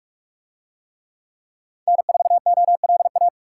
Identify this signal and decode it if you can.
Morse code, heard as a keyed tone, about 42 wpm, 700 Hz. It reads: N4OLA